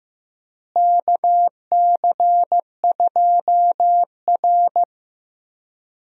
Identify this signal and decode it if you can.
Morse code, heard as a keyed tone, about 16 wpm, 700 Hz. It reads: KC2R
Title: USA Call Signs - 4 Characters - Repeated 2x - 15wpm